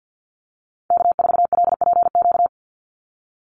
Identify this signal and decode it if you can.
Morse code, heard as a keyed tone, about 50 wpm, 700 Hz. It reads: K4LPQ
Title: NA SOTA Activator Callsigns 50wpm